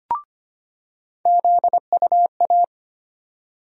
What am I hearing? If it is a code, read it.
Morse code, heard as a keyed tone, about 25 wpm, 700 Hz. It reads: ZUA
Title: USA Call Sign Postfixes - 3 Letters 25wpm